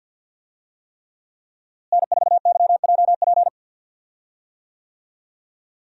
Morse code, heard as a keyed tone, about 52 wpm, 700 Hz. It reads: N4YJP
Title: USA Call Signs - 5 Characters - Repeated 3x - 50wpm